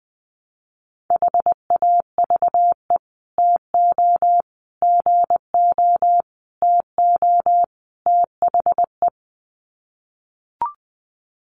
Morse code, heard as a keyed tone, about 20 wpm, 700 Hz. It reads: HAVE TO GO TO THE
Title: Sets of 5 Words Encore 20wpm